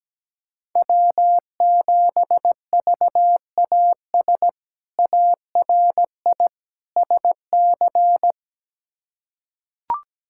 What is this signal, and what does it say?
Morse code, heard as a keyed tone, about 17 wpm, 700 Hz. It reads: W7VAS ARI SC
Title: K1USN SST QSO Practice - 17wpm